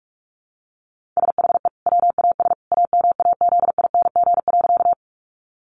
Morse code, heard as a keyed tone, about 44 wpm, 700 Hz. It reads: SHE WAS AMAZING.
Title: Sentences from Top 1500 Words 45wpm